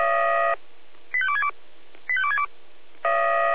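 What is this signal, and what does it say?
Five-tone selective call (ZVEI), heard as quick runs of five tones, tones of about 70 ms. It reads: 85262 85262
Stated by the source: ZVEI five-tone selective call received via SDR.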